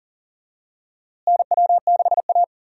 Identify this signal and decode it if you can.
Morse code, heard as a keyed tone, about 37 wpm, 700 Hz. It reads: NW6A
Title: USA Call Signs - 4 Characters - Repeated 2x - 40wpm